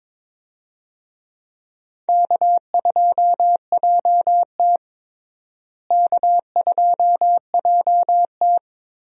Morse code, heard as a keyed tone, about 22 wpm, 700 Hz. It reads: K2JT K2JT
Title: USA Call Signs - 4 Characters - Repeated 3x - 22wpm